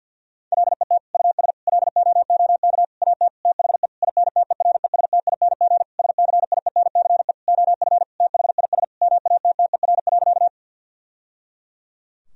Morse code, heard as a keyed tone, about 52 wpm, 700 Hz. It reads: LET US LOOK AT THE INTERESTING SCIENCE OF THIS MATTER.
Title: Sentences from Top 1000 Words - Review Part 2 - 50wpm